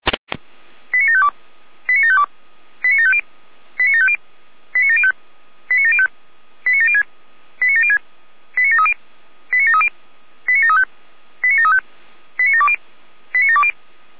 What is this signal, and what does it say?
Five-tone selective call (ZVEI), heard as quick runs of five tones, tones of about 70 ms. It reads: 89752 89752 89750 89750 89705 89705 89706 89706 89730 89730 89735 89735 89720 89720